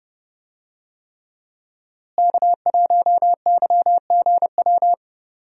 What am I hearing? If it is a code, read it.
Morse code, heard as a keyed tone, about 30 wpm, 700 Hz. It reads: K1YGW
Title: USA Call Signs - 5 Characters - Repeated 3x - 30wpm